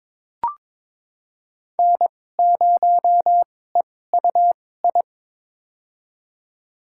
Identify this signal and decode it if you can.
Morse code, heard as a keyed tone, about 22 wpm, spaced as for 16 wpm, 690 Hz. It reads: N0EUI
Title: USA Call Signs - 5 Characters - 2x Character Spacing - 22wpm